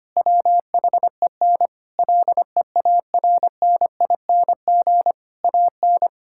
Morse code, heard as a keyed tone, about 25 wpm, 700 Hz. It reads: WHEN LEARNING AN
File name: practice / 25wpm / Difficult Sentences with Hints - 25wpm